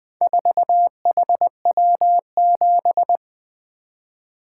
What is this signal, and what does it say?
Morse code, heard as a keyed tone, about 20 wpm, 700 Hz. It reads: 4HW7
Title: Four Letter-Number Combinations - 20wpm